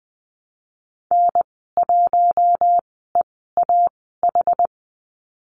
Morse code, heard as a keyed tone, about 19 wpm, spaced as for 14 wpm, 700 Hz. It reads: N1EAH
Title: 082 Lesson 14g - Number 2 - Call Signs 20wpm